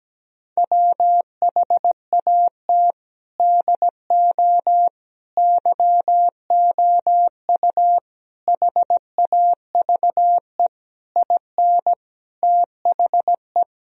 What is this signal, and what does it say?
Morse code, heard as a keyed tone, about 17 wpm, 700 Hz. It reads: WHAT DO YOU HAVE IN THE
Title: Sentences from Top 1000 Words - Review Part 2 - 17wpm